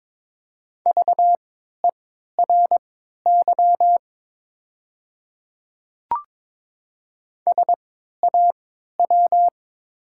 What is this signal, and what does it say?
Morse code, heard as a keyed tone, about 22 wpm, spaced as for 12 wpm, 700 Hz. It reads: VERY SAW
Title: 3x Character Spacing - Top 500 Words - 22wpm